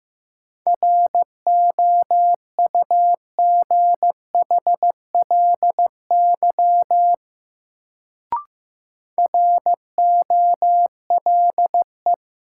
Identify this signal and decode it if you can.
Morse code, heard as a keyed tone, about 15 wpm, 700 Hz. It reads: ROUGHLY ROLE
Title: Sentences from Top 1500 Words - New Word Drill Alt - 15wpm